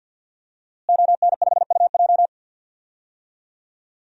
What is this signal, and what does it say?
Morse code, heard as a keyed tone, about 52 wpm, 700 Hz. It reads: ON5UJ
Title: DXCC-100 Call Signs - 5 Characters - Repeated 2x - 50wpm